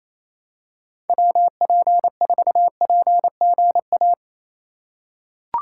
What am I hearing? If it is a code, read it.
Morse code, heard as a keyed tone, about 28 wpm, 700 Hz. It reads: WP4PGA
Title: USA Call Signs - 6 Characters - 3x Character Spacing - 28wpm